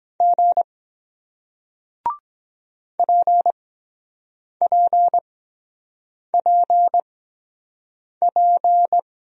Morse code, heard as a keyed tone, about 23 wpm, spaced as for 7 wpm, 700 Hz. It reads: G PPPP